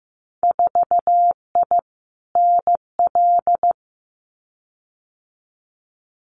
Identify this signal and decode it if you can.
Morse code, heard as a keyed tone, about 15 wpm, 700 Hz. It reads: 4I NL
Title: Winter Field Day Exchanges Encore 15wpm